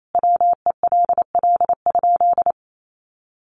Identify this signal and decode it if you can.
Morse code, heard as a keyed tone, about 28 wpm, 700 Hz. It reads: WELL?
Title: Sentences from Top 600 Words 28wpm